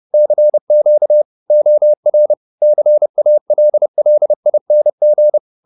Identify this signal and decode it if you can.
Morse code, heard as a keyed tone, about 30 wpm, 600 Hz. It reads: CQ OR CALLING